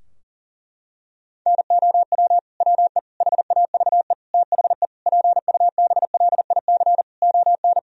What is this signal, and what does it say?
Morse code, heard as a keyed tone, about 40 wpm, 700 Hz. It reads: NOW WE HAVE THE PUBLIC ON
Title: Sentences from Top 1000 Words - Review Part 3 - 40wpm